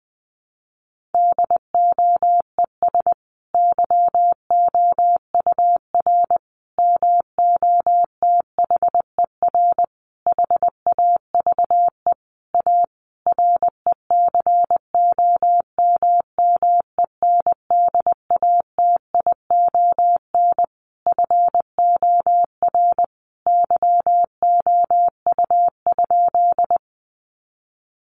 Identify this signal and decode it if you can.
Morse code, heard as a keyed tone, about 20 wpm, 700 Hz. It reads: DOES YOUR MOTHER HAVE A RECOMMENDATION FOR YOU?
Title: Sentences from Top 1400 Words 20wpm